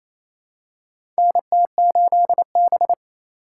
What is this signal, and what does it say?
Morse code, heard as a keyed tone, about 28 wpm, 700 Hz. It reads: NT8B
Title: USA Call Signs - 4 Characters - 2x Character Spacing - 28wpm